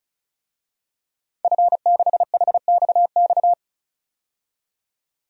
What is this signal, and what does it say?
Morse code, heard as a keyed tone, about 36 wpm, 700 Hz. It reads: F6HXX
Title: DXCC-100 Call Signs - 5 Characters - 35wpm